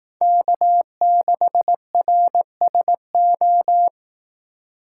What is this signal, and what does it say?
Morse code, heard as a keyed tone, about 18 wpm, 700 Hz. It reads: K6RSO
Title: Speed Racing - USA 1x3 Call Signs 15wpm